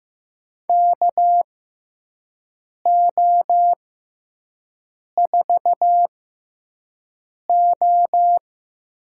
Morse code, heard as a keyed tone, about 15 wpm, spaced as for 5 wpm, 700 Hz. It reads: KO4O
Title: USA Call Signs - 5 Characters - 6x Character Spacing - 15wpm